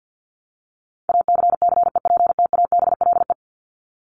Morse code, heard as a keyed tone, about 50 wpm, 700 Hz. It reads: ACCEPTABLE
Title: Top 1501-1750 Words 50wpm